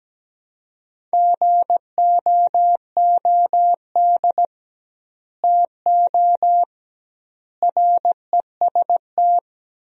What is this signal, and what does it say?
Morse code, heard as a keyed tone, about 17 wpm, 700 Hz. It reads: GOOD TO REST
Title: 2x Word-Spacing - Sentences from Top 500 Words - 17wpm